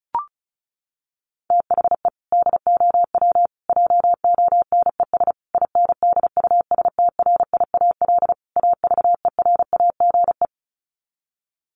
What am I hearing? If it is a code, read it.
Morse code, heard as a keyed tone, about 34 wpm, 700 Hz. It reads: THE DOW JONES INDUSTRIAL AVERAGE
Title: Sets of 5 Words 35wpm